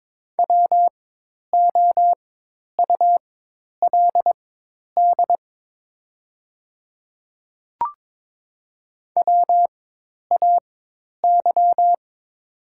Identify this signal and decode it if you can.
Morse code, heard as a keyed tone, about 22 wpm, spaced as for 10 wpm, 700 Hz. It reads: WOULD WAY
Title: 4x Character Spacing - Top 100 Words - 22wpm